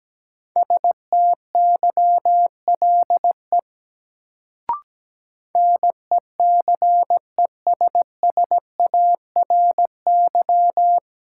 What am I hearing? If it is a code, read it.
Morse code, heard as a keyed tone, about 17 wpm, 700 Hz. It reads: STYLE NECESSARY
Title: Sentences from Top 800 Words - New Word Drill - 17wpm